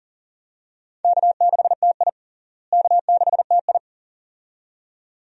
Morse code, heard as a keyed tone, about 37 wpm, 700 Hz. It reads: K6TI K6TI
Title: USA Call Signs - 4 Characters - Repeated 2x - 40wpm